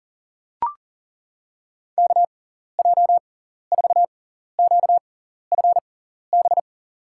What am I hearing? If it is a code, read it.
Morse code, heard as a keyed tone, about 40 wpm, spaced as for 14 wpm, 700 Hz. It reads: KJ4QFB
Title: USA Call Signs - 6x Character Spacing - 40wpm